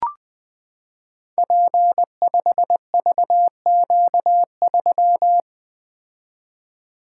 Morse code, heard as a keyed tone, about 20 wpm, 700 Hz. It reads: P5VQ3
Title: Code Groups - 5 Letter:Numbers - 20wpm